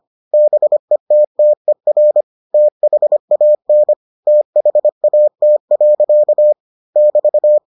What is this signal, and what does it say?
Morse code, heard as a keyed tone, about 25 wpm, 600 Hz. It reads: BETTER THAN THAT. <BT>